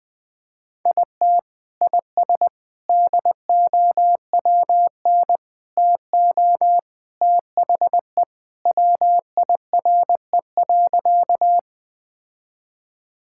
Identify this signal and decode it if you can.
Morse code, heard as a keyed tone, about 21 wpm, 700 Hz. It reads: IT IS DOWN TO THE WIRE.
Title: Sentences with 3-Word Phrases - 20wpm